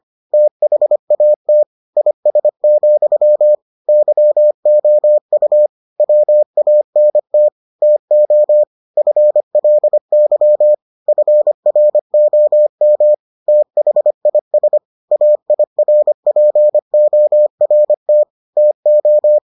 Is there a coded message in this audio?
Yes: Morse code from 0.3 to 19.4 s, heard as a keyed tone, about 25 wpm, 600 Hz. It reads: THAT IS, YOU WANT TO FLY FROM THIS AIRPORT TO